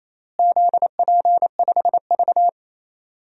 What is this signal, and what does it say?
Morse code, heard as a keyed tone, about 28 wpm, 700 Hz. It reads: ZP5V
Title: DXCC-100 Call Signs - 4 Characters - Repeated 3x - 28wpm